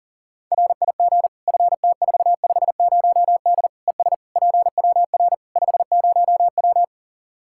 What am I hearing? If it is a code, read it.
Morse code, heard as a keyed tone, about 37 wpm, 700 Hz. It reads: RIG FT450D ES PWR 50W
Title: K1USN SST QSO Practice 40wpm